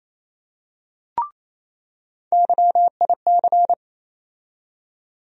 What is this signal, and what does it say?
Morse code, heard as a keyed tone, about 28 wpm, 700 Hz. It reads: YIC
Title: USA Call Sign Postfixes - 3 Letters 28wpm